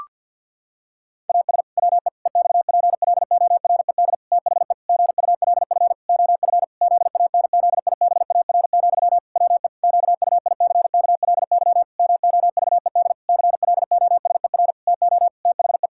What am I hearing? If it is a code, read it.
Morse code, heard as a keyed tone, about 50 wpm, 700 Hz. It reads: AS WE EXPLORED THE GULF OF ZANZIBAR, WE QUICKLY MOVED CLOSER TO THE
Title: Pangram Challenge v2 - 50wpm